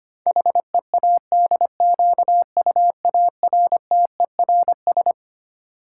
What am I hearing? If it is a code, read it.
Morse code, heard as a keyed tone, about 25 wpm, 700 Hz. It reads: HEADQUARTERS
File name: Twelve Letter Words - 25wpm